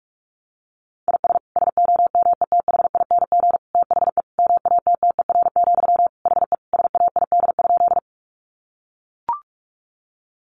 Morse code, heard as a keyed tone, about 44 wpm, 700 Hz. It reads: IS SOMETHING THE MATTER, HE SAID?
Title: Sentences from Top 700 Words 45wpm